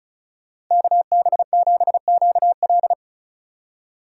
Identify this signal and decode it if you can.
Morse code, heard as a keyed tone, about 43 wpm, 700 Hz. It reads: KB7QL
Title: Speed Racing - USA 2x2 Call Signs 35wpm